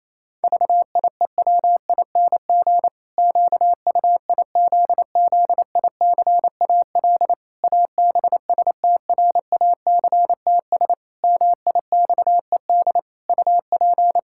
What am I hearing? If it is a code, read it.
Morse code, heard as a keyed tone, about 28 wpm, 700 Hz. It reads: VIEWING QUIZZICAL ABSTRACTS MIXED UP
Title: Pangram Challenge v2 - 28wpm